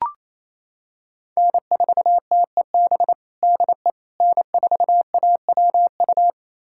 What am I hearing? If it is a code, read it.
Morse code, heard as a keyed tone, about 28 wpm, 700 Hz. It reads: N4TEB DE N4AWU
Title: USA Callsign DE Callsign - 5 Characters - 28wpm